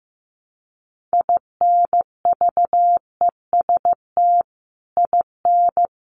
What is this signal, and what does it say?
Morse code, heard as a keyed tone, about 15 wpm, 700 Hz. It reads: INVEST IN
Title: Sentences from Top 1100 Words 15wpm